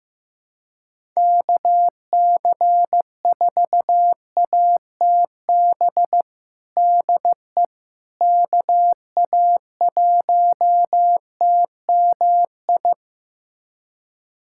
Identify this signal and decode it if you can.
Morse code, heard as a keyed tone, about 15 wpm, 700 Hz. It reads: KC4ATB DE KA1TMI
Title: USA Callsign DE Callsign - 6 Characters - 15wpm